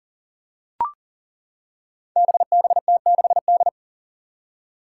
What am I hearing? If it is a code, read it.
Morse code, heard as a keyed tone, about 39 wpm, 700 Hz. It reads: BBT6D
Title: Five Challenging Letter-Number Combinations 40wpm